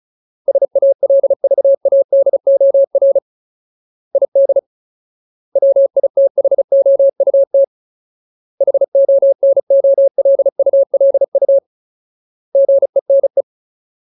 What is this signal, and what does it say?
Morse code, heard as a keyed tone, about 34 wpm, 550 Hz. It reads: SALVADOR ID WITHOUT HONOLULU GENE
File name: Ultimate Word List / 35WPM Ultimate